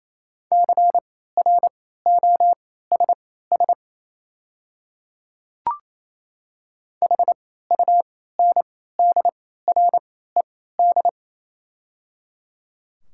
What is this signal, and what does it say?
Morse code, heard as a keyed tone, about 28 wpm, spaced as for 16 wpm, 700 Hz. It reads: CROSS HUNDRED
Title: 3x Character Spacing - Top 500 Words - 28wpm